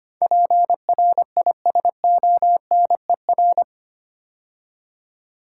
Morse code, heard as a keyed tone, about 25 wpm, 700 Hz. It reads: PRISONER
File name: 8 Letter Words - First 3 Letters Given - 25wpm